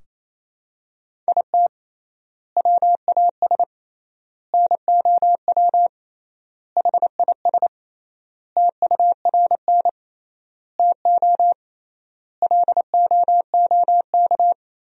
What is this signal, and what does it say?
Morse code, heard as a keyed tone, about 28 wpm, 700 Hz. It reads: IT WAS NOW HIS TURN TO LOOK
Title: 3x Word-Spacing - Sentences from Top 200 Words - 28wpm